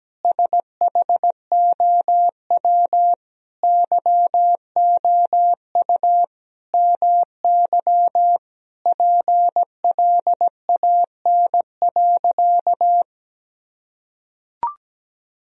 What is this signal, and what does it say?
Morse code, heard as a keyed tone, about 17 wpm, 700 Hz. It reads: SHOW YOU MY PLAN.
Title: Sentences from Top 1000 Words - Review Part 2 - 17wpm